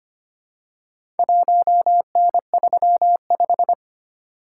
Morse code, heard as a keyed tone, about 25 wpm, 700 Hz. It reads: 1N35
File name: practice / 25wpm / Four Letter-Number Combinations - 25wpm